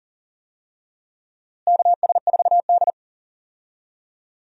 Morse code, heard as a keyed tone, about 40 wpm, 700 Hz. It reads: KS4D
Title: USA Call Signs - 4 Characters - Repeated 3x - 40wpm